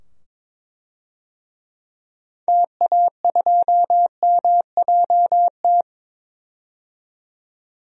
Morse code, heard as a keyed tone, about 21 wpm, 700 Hz. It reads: TA2MJT
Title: DXCC-100 Call Signs - 6 Characters - 22wpm